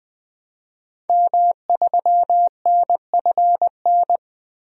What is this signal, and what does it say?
Morse code, heard as a keyed tone, about 20 wpm, 700 Hz. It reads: M3NFN
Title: DXCC-100 Call Signs - 5 Characters - 20wpm